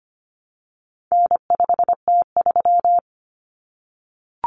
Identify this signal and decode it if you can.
Morse code, heard as a keyed tone, about 25 wpm, 700 Hz. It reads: N5T3
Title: Common Cut Numbers - 4-Digit 25wpm